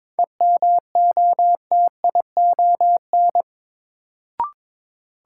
Morse code, heard as a keyed tone, about 22 wpm, 700 Hz. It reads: EMOTION
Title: Sentences from Top 1600 Words - New Word Drill - 22wpm